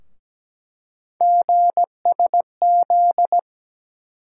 Morse code, heard as a keyed tone, about 17 wpm, 700 Hz. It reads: GSZ
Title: USA Call Sign Postfixes - 3 Letters 17wpm